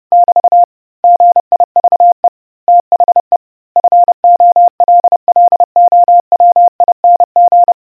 Morse code, heard as a keyed tone, about 30 wpm, 700 Hz. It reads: <BT> GIVE THE FOLLOWING